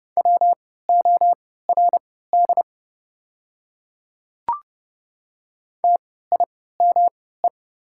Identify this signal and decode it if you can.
Morse code, heard as a keyed tone, about 31 wpm, spaced as for 17 wpm, 690 Hz. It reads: WORD TIME
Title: 3x Character Spacing - Top 100 Words - 30wpm